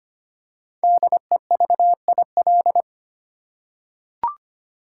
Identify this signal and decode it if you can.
Morse code, heard as a keyed tone, about 25 wpm, 700 Hz. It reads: DEVIL
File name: 5 Letter Words - First 3 Letters Given - 25wpm